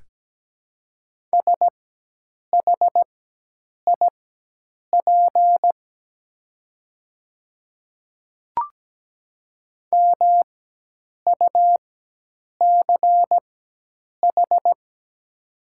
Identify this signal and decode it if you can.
Morse code, heard as a keyed tone, about 17 wpm, spaced as for 8 wpm, 700 Hz. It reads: SHIP MUCH
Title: 4x Character Spacing - Top 500 Words - 17wpm